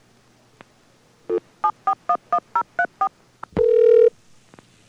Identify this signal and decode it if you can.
DTMF, heard as telephone touch tones, tones of about 80 ms. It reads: 7411034
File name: acc / acc-rc-850-and-rc-85-demo